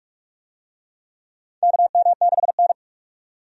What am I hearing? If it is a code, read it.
Morse code, heard as a keyed tone, about 46 wpm, 700 Hz. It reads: KM6N
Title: USA Call Signs - 4 Characters - 45wpm